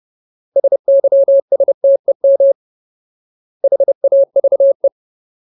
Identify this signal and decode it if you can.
Morse code, heard as a keyed tone, about 30 wpm, 550 Hz. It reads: SYSTEM HAVE
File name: Ultimate Word List / 30WPM Ultimate